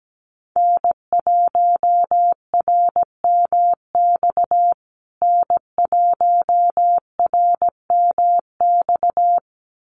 Morse code, heard as a keyed tone, about 17 wpm, 700 Hz. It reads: N1RMX N1RMX
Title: USA Call Signs Repeated 2x 17wpm